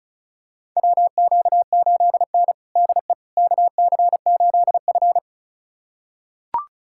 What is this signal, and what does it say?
Morse code, heard as a keyed tone, about 34 wpm, 700 Hz. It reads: WQ8N DE KC8F
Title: USA Callsign DE Callsign - 4 Characters - 35wpm